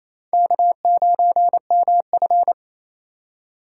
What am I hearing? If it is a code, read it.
Morse code, heard as a keyed tone, about 37 wpm, 700 Hz. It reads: K9MF